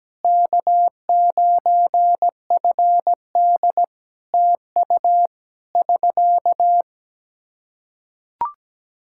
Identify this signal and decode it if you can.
Morse code, heard as a keyed tone, about 17 wpm, 700 Hz. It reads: K9FD TU <SK>
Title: K1USN SST QSO Practice 17wpm